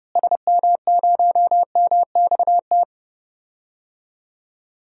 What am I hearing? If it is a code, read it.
Morse code, heard as a keyed tone, about 30 wpm, 700 Hz. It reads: SM0MXT
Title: DXCC-100 Call Signs - 6 Characters - Repeated 3x - 30wpm